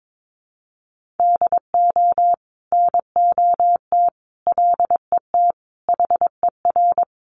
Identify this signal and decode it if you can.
Morse code, heard as a keyed tone, about 22 wpm, 700 Hz. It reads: DO NOT LET HER
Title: Sentences from Top 1300 Words 22wpm